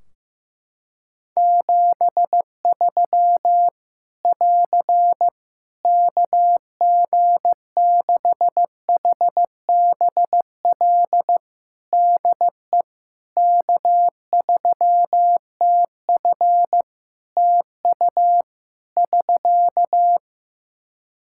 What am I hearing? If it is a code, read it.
Morse code, heard as a keyed tone, about 15 wpm, 700 Hz. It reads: 73 <AR> KG6HBL DE K3TF TU <SK>
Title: Practice QSO - Exchange 4 - Ending - 15wpm